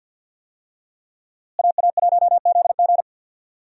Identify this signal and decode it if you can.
Morse code, heard as a keyed tone, about 53 wpm, 700 Hz. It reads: AA1ZG
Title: USA Call Signs - 5 Characters - Repeated 3x - 50wpm